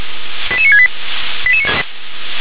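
Five-tone selective call (ZVEI), heard as quick runs of five tones, tones of about 70 ms. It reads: 88968
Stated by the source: ZVEI five-tone selective call received with an SDR.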